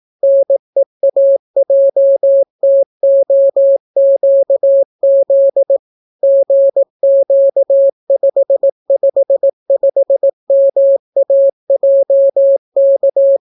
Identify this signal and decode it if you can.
Morse code, heard as a keyed tone, about 18 wpm, 550 Hz. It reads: NEAJTOQZ GQ555MAJK